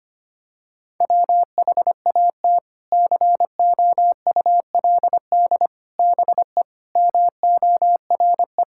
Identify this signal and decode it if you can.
Morse code, heard as a keyed tone, about 25 wpm, 700 Hz. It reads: WHAT COULD BE MORE
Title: Sentences from Top 1000 Words - Review Part 6 - 25wpm